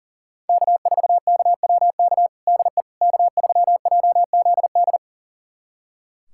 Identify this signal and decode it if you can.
Morse code, heard as a keyed tone, about 40 wpm, 700 Hz. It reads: K4KWK DE K3JZD
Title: USA Callsign DE Callsign - 5 Characters - 40wpm